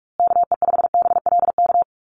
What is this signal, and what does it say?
Morse code, heard as a keyed tone, about 44 wpm, 700 Hz. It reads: KE5BLK
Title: USA Call Signs Repeated 2x 45wpm